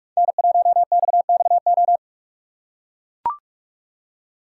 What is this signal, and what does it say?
Morse code, heard as a keyed tone, about 44 wpm, 700 Hz. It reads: N1XXO
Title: USA Call Signs - 5 Characters - 6x Character Spacing - 45wpm